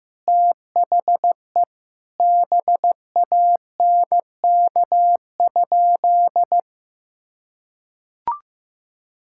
Morse code, heard as a keyed tone, about 16 wpm, 700 Hz. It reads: THE BANK?
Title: Sentences from Top 1000 Words - Review Part 6 - 15wpm